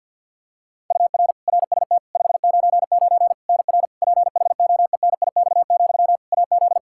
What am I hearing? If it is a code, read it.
Morse code, heard as a keyed tone, about 51 wpm, 700 Hz. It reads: UR RST 599 NR PHOENIX, AZ
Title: Practice QSO - Exchange 1 - Beginning - 50wpm